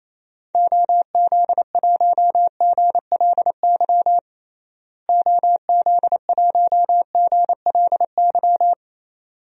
Morse code, heard as a keyed tone, about 28 wpm, 700 Hz. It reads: OZ1GLY OZ1GLY
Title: DXCC-100 Call Signs - 6 Characters - Repeated 2x - 28wpm